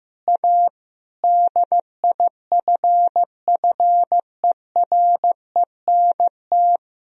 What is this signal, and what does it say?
Morse code, heard as a keyed tone, about 15 wpm, 700 Hz. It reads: A DIFFERENT
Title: Sentences from Top 1000 Words - Review Part 3 - 15wpm